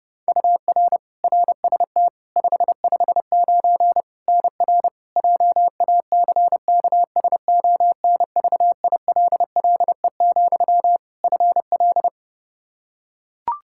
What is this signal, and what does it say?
Morse code, heard as a keyed tone, about 30 wpm, 700 Hz. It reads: UR RST 559 NR JACKSONVILLE, FL
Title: Practice QSO - Exchange 1 - Beginning - 30wpm